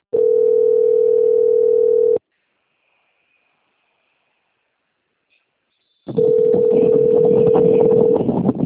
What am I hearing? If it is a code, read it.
Ringback tone (North America), 440 + 480 Hz.
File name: out- - - - - .17679